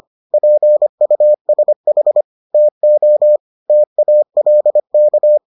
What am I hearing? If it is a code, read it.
Morse code, heard as a keyed tone, about 25 wpm, 600 Hz. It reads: PUSH TO TALK